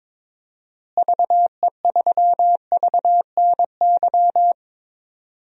Morse code, heard as a keyed tone, about 22 wpm, 700 Hz. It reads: VE3VNY
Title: DXCC-100 Call Signs - 6 Characters - Repeated 3x - 22wpm